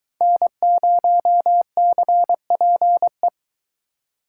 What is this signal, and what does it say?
Morse code, heard as a keyed tone, about 22 wpm, 700 Hz. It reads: N0CPE